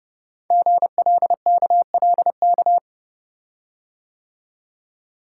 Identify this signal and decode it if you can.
Morse code, heard as a keyed tone, about 30 wpm, 700 Hz. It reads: GLKLK
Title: Code Groups - 5 Letters - 30wpm